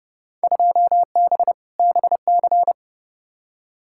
Morse code, heard as a keyed tone, about 29 wpm, 700 Hz. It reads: 2B BC
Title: Field Day Exchanges Encore - 30wpm